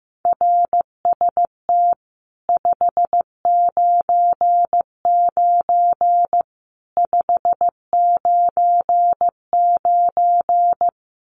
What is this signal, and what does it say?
Morse code, heard as a keyed tone, about 15 wpm, 700 Hz. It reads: RST 599 599
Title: Common QSO Phrases 15wpm